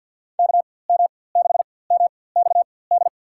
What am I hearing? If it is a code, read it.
Morse code, heard as a keyed tone, about 51 wpm, spaced as for 23 wpm, 700 Hz. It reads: KM6MXD
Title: USA Call Signs - 6 Characters - 4x Character Spacing - 50wpm